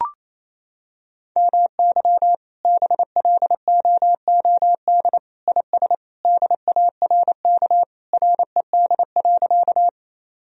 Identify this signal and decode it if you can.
Morse code, heard as a keyed tone, about 27 wpm, 700 Hz. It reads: MY BLOOD IS DARK RED.
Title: Sentences from Top 1000 Words - Review Part 2 - 28wpm